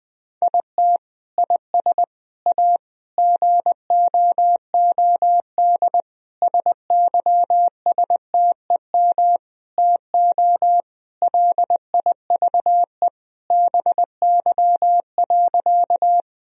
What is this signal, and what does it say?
Morse code, heard as a keyed tone, about 20 wpm, 700 Hz. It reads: IT IS A GOOD SYSTEM TO LIVE BY.
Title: Sentences from Top 1000 Words - Review Part 3 - 20wpm